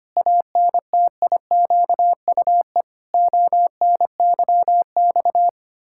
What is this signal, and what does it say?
Morse code, heard as a keyed tone, about 25 wpm, 700 Hz. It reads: ANTIQUE ONYX
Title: Pangram Challenge v2 - 25wpm